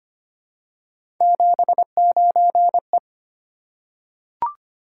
Morse code, heard as a keyed tone, about 25 wpm, 700 Hz. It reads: 79E
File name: Three Letter-Number Combinations - 25wpm